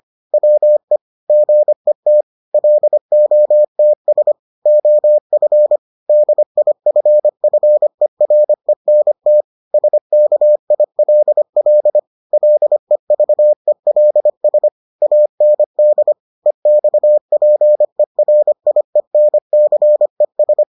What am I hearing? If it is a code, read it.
Morse code, heard as a keyed tone, about 25 wpm, 600 Hz. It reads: WE GET LOTS OF DIFFERENT SKILL LEVELS AND EXPERIENCES